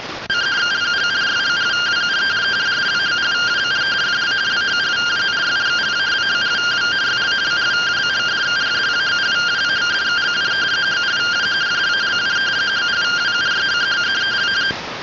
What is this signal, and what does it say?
Two-tone teleprinter signal, about 170 Hz shift.